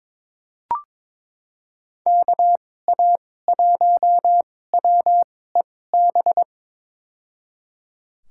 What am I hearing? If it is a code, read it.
Morse code, heard as a keyed tone, about 22 wpm, spaced as for 16 wpm, 700 Hz. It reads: KA1WEB
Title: USA Call Signs - 6 Characters - 2x Character Spacing - 22wpm